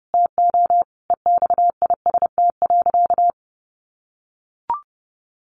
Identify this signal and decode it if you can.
Morse code, heard as a keyed tone, about 30 wpm, 700 Hz. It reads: TO EXIST.
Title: Sentences from Top 1400 Words 30wpm